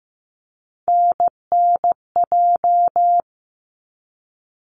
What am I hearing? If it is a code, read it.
Morse code, heard as a keyed tone, about 15 wpm, 700 Hz. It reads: NNJ
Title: ARRL Sections 15wpm